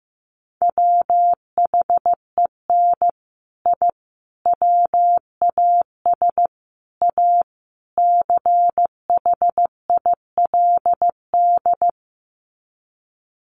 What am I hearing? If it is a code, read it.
Morse code, heard as a keyed tone, about 15 wpm, 700 Hz. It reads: WHEN I WAS A CHILD
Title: Sets of 5 Words Encore 15wpm